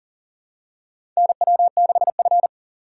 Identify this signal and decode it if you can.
Morse code, heard as a keyed tone, about 40 wpm, 700 Hz. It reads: NW6F
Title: USA Call Signs - 4 Characters - 2x Character Spacing - 40wpm